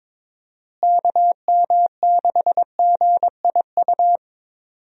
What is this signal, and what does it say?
Morse code, heard as a keyed tone, about 22 wpm, 700 Hz. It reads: KM6GIU
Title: USA Call Signs - 6 Characters - Repeated 2x - 22wpm